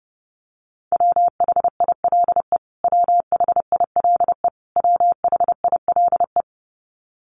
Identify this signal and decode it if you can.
Morse code, heard as a keyed tone, about 29 wpm, 700 Hz. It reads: WHILE WHILE WHILE
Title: Top 300 Words Repeated 30wpm